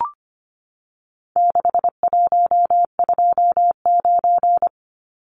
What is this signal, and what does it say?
Morse code, heard as a keyed tone, about 25 wpm, 700 Hz. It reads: 6129
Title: Numbers - 4-Digit 25wpm